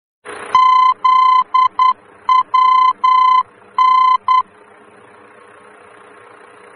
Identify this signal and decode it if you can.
Morse code, heard as a keyed tone, about 10 wpm, 1020 Hz. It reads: ZWN
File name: germany / mx / ZWN-D-VOR